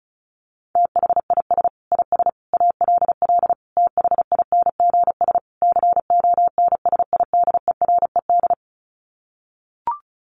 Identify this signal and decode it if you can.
Morse code, heard as a keyed tone, about 34 wpm, 700 Hz. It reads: THIS IS ALL THINGS CONSIDERED
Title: Sets of 5 Words 35wpm